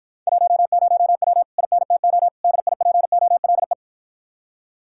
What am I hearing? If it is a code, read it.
Morse code, heard as a keyed tone, about 52 wpm, 700 Hz. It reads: 10W INTO DIPOLE